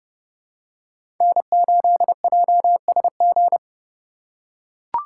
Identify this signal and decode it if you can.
Morse code, heard as a keyed tone, about 32 wpm, 700 Hz. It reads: N8JSG
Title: USA Call Signs - 5 Characters - 4x Character Spacing - 30wpm